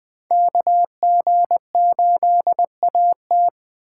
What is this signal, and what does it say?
Morse code, heard as a keyed tone, about 20 wpm, 700 Hz. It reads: KG8AT